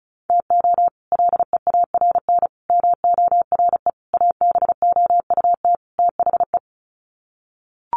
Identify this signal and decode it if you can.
Morse code, heard as a keyed tone, about 34 wpm, 700 Hz. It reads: TO LEARN MORE ABOUT THE
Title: Sets of 5 Words Encore2 35wpm